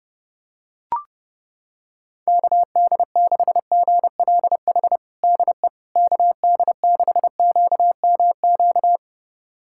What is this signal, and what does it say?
Morse code, heard as a keyed tone, about 30 wpm, 700 Hz. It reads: KD6GLH DE KD6QMQ
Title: USA Callsign DE Callsign - 6 Characters - 30wpm